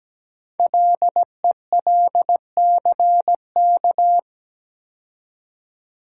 Morse code, heard as a keyed tone, about 17 wpm, 700 Hz. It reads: LELCK
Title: Code Groups - 5 Letters - 17wpm